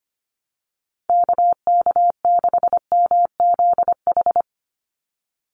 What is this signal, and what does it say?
Morse code, heard as a keyed tone, about 25 wpm, 700 Hz. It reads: KK6MZH
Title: USA Call Signs 25wpm